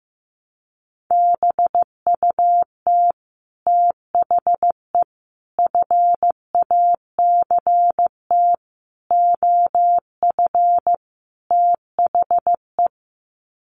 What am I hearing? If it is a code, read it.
Morse code, heard as a keyed tone, about 15 wpm, 700 Hz. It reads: BUT THE FACT OF THE
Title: Sets of 5 Words Encore2 15wpm